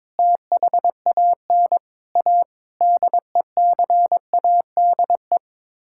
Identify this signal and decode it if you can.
Morse code, heard as a keyed tone, about 22 wpm, 700 Hz. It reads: THAN A DECADE
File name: Sets of 9 Words - 22wpm